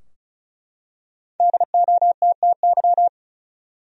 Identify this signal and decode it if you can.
Morse code, heard as a keyed tone, about 36 wpm, 700 Hz. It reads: DOTTY
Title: USA Common Names - Max 5 Letters - 35wpm